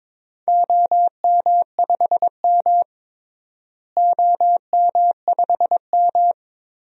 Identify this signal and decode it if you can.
Morse code, heard as a keyed tone, about 22 wpm, 700 Hz. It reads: OM5M OM5M
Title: DXCC-100 Call Signs - 4 Characters - Repeated 2x - 22wpm